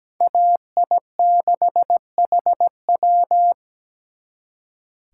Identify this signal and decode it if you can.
Morse code, heard as a keyed tone, about 20 wpm, 700 Hz. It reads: AI6HW
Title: Speed Racing - USA 2x2 Call Signs 17wpm